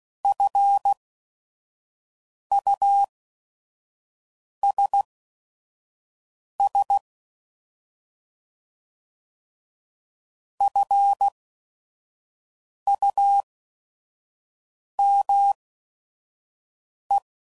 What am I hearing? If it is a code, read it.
Morse code, heard as a keyed tone, about 16 wpm, spaced as for 5 wpm, 800 Hz. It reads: FUSS FUME
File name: Prefixes / By First Letter / Fs